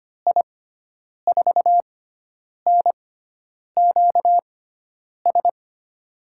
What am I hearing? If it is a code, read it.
Morse code, heard as a keyed tone, about 25 wpm, spaced as for 9 wpm, 700 Hz. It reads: I4NQS
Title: USA Call Signs - 6 Characters - 6x Character Spacing - 25wpm